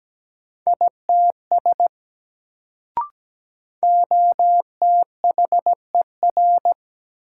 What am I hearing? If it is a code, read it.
Morse code, heard as a keyed tone, about 17 wpm, 700 Hz. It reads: ITS OTHER
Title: Sets of 2 Words - New Word Drill - 17wpm